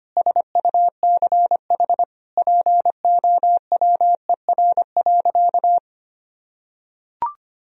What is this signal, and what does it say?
Morse code, heard as a keyed tone, about 25 wpm, 700 Hz. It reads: SUCH POWER.
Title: Sentences from Top 1000 Words - Review Part 2 - 25wpm